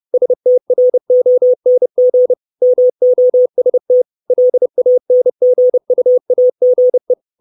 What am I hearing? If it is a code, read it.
Morse code, heard as a keyed tone, about 30 wpm, 500 Hz. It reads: STRONG MOST LANGUAGE
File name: 632 Words 30WPM